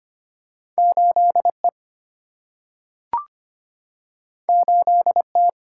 Morse code, heard as a keyed tone, about 25 wpm, 700 Hz. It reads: 8E 8T
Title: Single-Letter Single-Number 25wpm